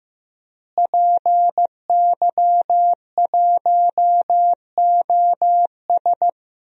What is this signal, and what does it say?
Morse code, heard as a keyed tone, about 15 wpm, 700 Hz. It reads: PY1OS
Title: DXCC-100 Call Signs - 5 Characters - Repeated 3x - 15wpm